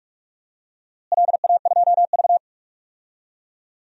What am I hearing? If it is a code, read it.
Morse code, heard as a keyed tone, about 41 wpm, 700 Hz. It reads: LA2V
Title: DXCC-100 Call Signs - 4 Characters - Repeated 3x - 45wpm